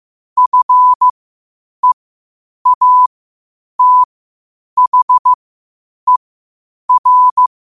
Morse code, heard as a keyed tone, about 15 wpm, spaced as for 8 wpm, 1000 Hz. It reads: FEATHER
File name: qotdfiles / 8wpm